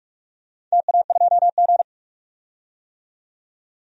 Morse code, heard as a keyed tone, about 44 wpm, 690 Hz. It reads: TA2G
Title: DXCC-100 Call Signs - 4 Characters - 45wpm